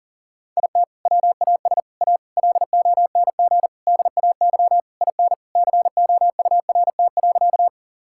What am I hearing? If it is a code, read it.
Morse code, heard as a keyed tone, about 40 wpm, 700 Hz. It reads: IT WAS A LONG DAY IN COURT.
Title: Sentences from Top 1600 Words 40wpm